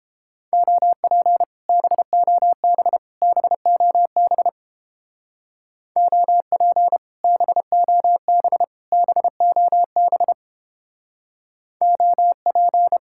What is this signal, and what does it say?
Morse code, heard as a keyed tone, about 29 wpm, 700 Hz. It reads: OP BOB BOB OP BOB BOB OP